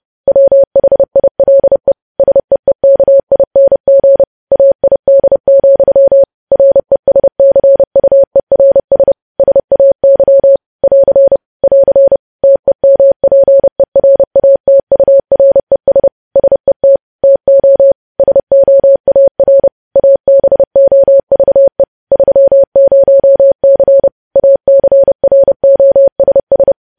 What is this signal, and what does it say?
Morse code, heard as a keyed tone, about 30 wpm, 550 Hz. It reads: WHILE SEEKING AID, RESCUERS SAY <AR> <AR> TEMPERATURES SET TO SOAR ABOVE 30C ACROSS
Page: news headlines in morse code at 30 wpm